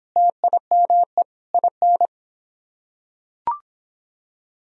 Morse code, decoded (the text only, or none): TIME IN